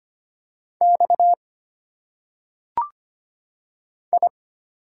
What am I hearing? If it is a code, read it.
Morse code, heard as a keyed tone, about 25 wpm, 700 Hz. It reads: X I
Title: Single Letters - Phonetic Pronunciation 25wpm